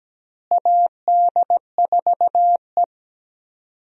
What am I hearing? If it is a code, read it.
Morse code, heard as a keyed tone, about 17 wpm, 700 Hz. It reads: AD4E
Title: USA Call Signs - 4 Characters - Repeated 2x - 17wpm